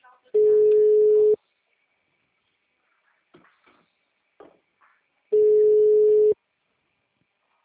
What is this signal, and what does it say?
Ringback tone (Europe, 425 Hz), with speech.
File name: out- -2005- - - .6931